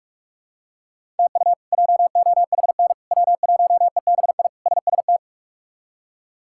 Morse code, heard as a keyed tone, about 46 wpm, 700 Hz. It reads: TU JOHN W1EBI SST
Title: K1USN SST QSO Practice - 45wpm